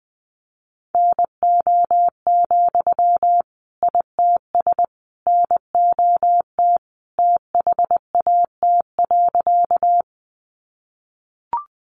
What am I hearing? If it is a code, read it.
Morse code, heard as a keyed tone, about 20 wpm, 700 Hz. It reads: NO, ITS NOT THAT.
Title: Sentences from Top 600 Words 20wpm